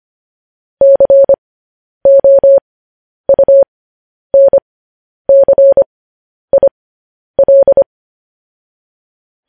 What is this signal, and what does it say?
Morse code, heard as a keyed tone, about 25 wpm, spaced as for 10 wpm, 550 Hz. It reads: COUNCIL
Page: news-headlines-in-morse-code-at-10-wpm